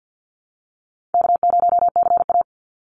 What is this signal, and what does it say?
Morse code, heard as a keyed tone, about 50 wpm, 700 Hz. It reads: K0CA
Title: Call Signs Repeated 2x - 50wpm